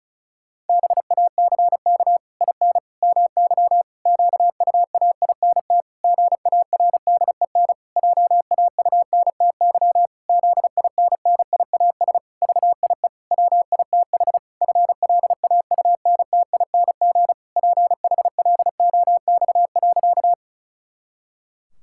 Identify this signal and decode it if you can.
Morse code, heard as a keyed tone, about 34 wpm, 700 Hz. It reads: BACK IN MY QUAINT GARDEN JAUNTY ZINNIAS VIE WITH FLAUNTING PHLOX.
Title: Pangram Challenge v2 - 35wpm